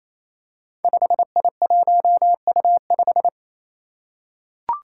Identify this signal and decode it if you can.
Morse code, heard as a keyed tone, about 28 wpm, 700 Hz. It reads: 5I1U5
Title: Five Challenging Letter-Number Combinations 28wpm